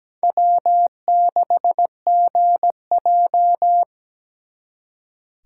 Morse code, heard as a keyed tone, about 20 wpm, 700 Hz. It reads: W6GJ
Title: Speed Racing - USA 1x2 Call Signs 17wpm